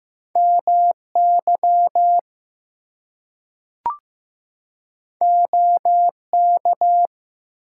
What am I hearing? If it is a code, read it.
Morse code, heard as a keyed tone, about 15 wpm, 700 Hz. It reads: MY OK
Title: Two Letter Words - 15wpm